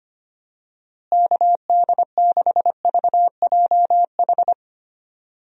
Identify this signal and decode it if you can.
Morse code, heard as a keyed tone, about 25 wpm, 700 Hz. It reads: KD6VJH
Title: USA Call Signs - 6 Characters - Repeated 2x - 25wpm